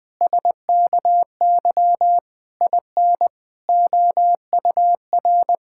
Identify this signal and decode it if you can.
Morse code, heard as a keyed tone, about 21 wpm, 700 Hz. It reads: SKY IN OUR
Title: Pangram Challenge v2 - 20wpm